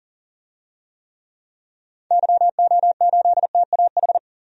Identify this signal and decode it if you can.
Morse code, heard as a keyed tone, about 40 wpm, 700 Hz. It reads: YO8TAH
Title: DXCC-100 Call Signs - 6 Characters - Repeated 2x - 40wpm